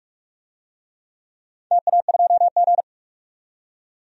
Morse code, heard as a keyed tone, about 42 wpm, 690 Hz. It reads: TA2G
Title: DXCC-100 Call Signs - 4 Characters - 45wpm